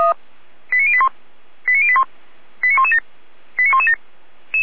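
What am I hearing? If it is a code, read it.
Five-tone selective call (ZVEI), heard as quick runs of five tones, tones of about 70 ms. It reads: 89081 89081 89117 89117
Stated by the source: ZVEI five-tone call 89115 received on SDR.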